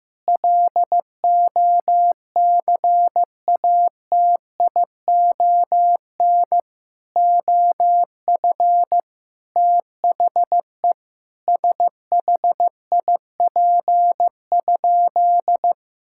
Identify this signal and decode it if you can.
Morse code, heard as a keyed tone, about 16 wpm, 700 Hz. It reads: LOCATION OF THE SHIP?
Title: Sentences from Top 1000 Words - Review Part 6 - 15wpm